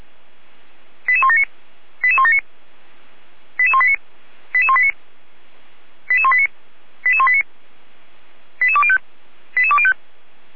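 Five-tone selective call (ZVEI), heard as quick runs of five tones, tones of about 70 ms. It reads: 88179 88179 88189 88189 88189 88189 88205 88205
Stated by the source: ZVEI five-tone selective call received by SDR, from the feuerwehr-orlamuende.org fire brigade audio archive.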